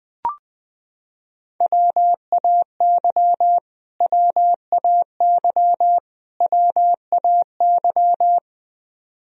Morse code, heard as a keyed tone, about 20 wpm, 700 Hz. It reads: WAY WAY WAY
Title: Top 100 Words Repeated 20wpm